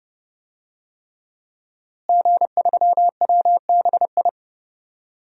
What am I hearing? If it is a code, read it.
Morse code, heard as a keyed tone, about 30 wpm, 700 Hz. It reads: G3WBI
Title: DXCC-100 Call Signs - 5 Characters - Repeated 3x - 30wpm